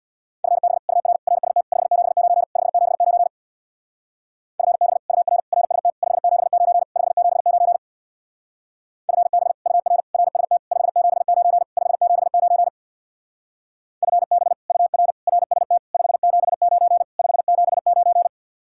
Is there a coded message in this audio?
Yes: Morse code from 0.4 to 18.3 s, heard as a keyed tone, about 65 wpm, 700 Hz. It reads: FB UR RST 579 579 FB UR RST 579 579 FB UR RST 579 579 FB UR RST 579 579